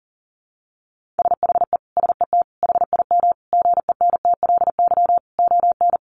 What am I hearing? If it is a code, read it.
Morse code, heard as a keyed tone, about 40 wpm, 700 Hz. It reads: SHE SET HIM GENTLY ON
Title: Sentences from Top 1500 Words 40wpm